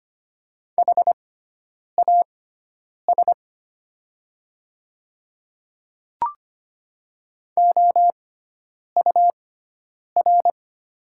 Morse code, heard as a keyed tone, about 25 wpm, spaced as for 9 wpm, 700 Hz. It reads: HAS OUR
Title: 6x Character Spacing - Top 200 Words - 25wpm